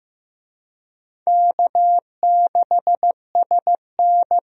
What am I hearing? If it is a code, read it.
Morse code, heard as a keyed tone, about 15 wpm, 700 Hz. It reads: K6SN
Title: USA Call Signs - 4 Characters - 15wpm